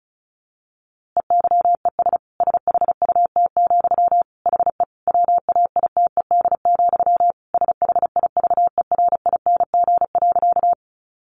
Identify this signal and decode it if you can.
Morse code, heard as a keyed tone, about 34 wpm, 700 Hz. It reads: EYES SHUT, HE WAITED, SHIVERING.
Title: A Sound Of Thunder 35wpm